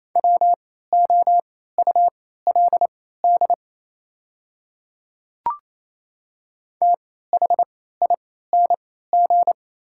Morse code, heard as a keyed tone, about 28 wpm, spaced as for 16 wpm, 700 Hz. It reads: WOULD THING
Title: 3x Character Spacing - Top 100 Words - 28wpm